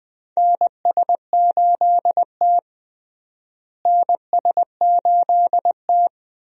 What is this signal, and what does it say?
Morse code, heard as a keyed tone, about 20 wpm, 700 Hz. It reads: NS8T NS8T
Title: USA Call Signs - 4 Characters - Repeated 2x - 20wpm